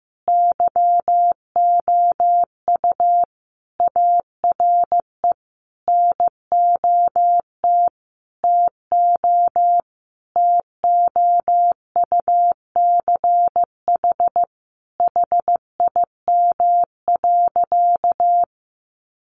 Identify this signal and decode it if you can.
Morse code, heard as a keyed tone, about 15 wpm, 700 Hz. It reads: YOU ARE NOT TO TOUCH HIM.
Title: Sentences from Top 1200 Words 15wpm